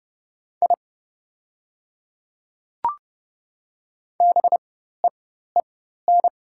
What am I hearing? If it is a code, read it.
Morse code, heard as a keyed tone, about 29 wpm, spaced as for 14 wpm, 700 Hz. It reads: I BEEN
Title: 4x Character Spacing - Top 100 Words - 30wpm